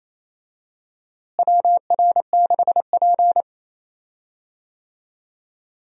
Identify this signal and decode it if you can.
Morse code, heard as a keyed tone, about 28 wpm, 700 Hz. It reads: WR6P
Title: USA Call Signs - 4 Characters - Repeated 2x - 28wpm